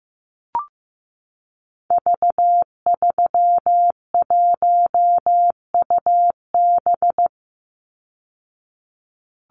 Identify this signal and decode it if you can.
Morse code, heard as a keyed tone, about 15 wpm, 700 Hz. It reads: V31UB
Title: Call Signs Repeated 2x - 15wpm